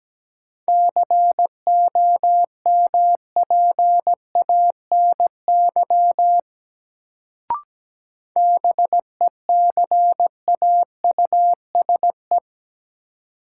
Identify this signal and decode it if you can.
Morse code, heard as a keyed tone, about 17 wpm, 700 Hz. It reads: COMPANY BECAUSE
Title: Sentences from Top 600 Words - New Word Drill - 17wpm